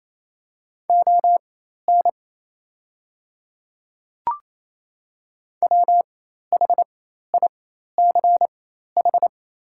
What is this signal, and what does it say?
Morse code, heard as a keyed tone, about 27 wpm, spaced as for 13 wpm, 690 Hz. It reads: ON WHICH
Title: 4x Character Spacing - Top 100 Words - 28wpm